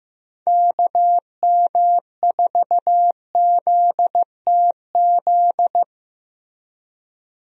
Morse code, heard as a keyed tone, about 15 wpm, 700 Hz. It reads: KM4ZTZ
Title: USA Call Signs - 6 Characters - Repeated 3x - 15wpm